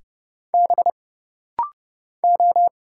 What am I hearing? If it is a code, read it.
Morse code, heard as a keyed tone, about 29 wpm, 700 Hz. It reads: B O
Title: Single Letters - Rapid-Fire 30wpm